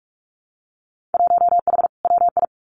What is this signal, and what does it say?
Morse code, heard as a keyed tone, about 46 wpm, 700 Hz. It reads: 1H WI
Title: Winter Field Day Exchanges 45wpm